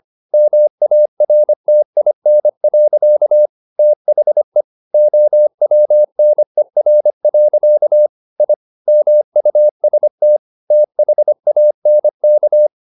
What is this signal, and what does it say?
Morse code, heard as a keyed tone, about 25 wpm, 600 Hz. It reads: MARTIN. THE OWNER. I MUST THANK
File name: the old man and the sea part 1 25wpm